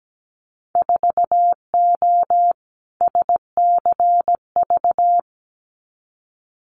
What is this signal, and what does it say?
Morse code, heard as a keyed tone, about 17 wpm, 700 Hz. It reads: 4O SCV
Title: Winter Field Day Exchanges 17wpm